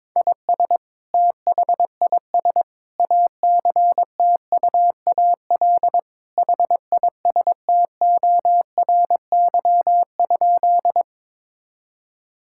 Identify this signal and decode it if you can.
Morse code, heard as a keyed tone, about 22 wpm, 700 Hz. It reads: IS THIS ACTUAL HISTORY?
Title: Sentences from Top 1000 Words - Review Part 5 - 22wpm